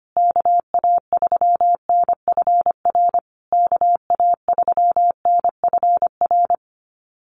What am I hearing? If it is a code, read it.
Morse code, heard as a keyed tone, about 25 wpm, 700 Hz. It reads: KA3NFR KA3NFR
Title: USA Call Signs Repeated 3x 25wpm